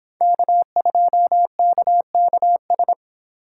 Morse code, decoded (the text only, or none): K2KKS